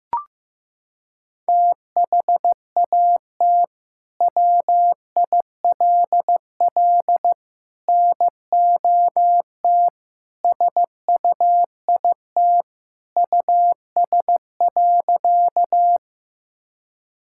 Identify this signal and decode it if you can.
Morse code, heard as a keyed tone, about 15 wpm, 700 Hz. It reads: THAT WILL NOT SUIT US.
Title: Sentences from Top 1600 Words 15wpm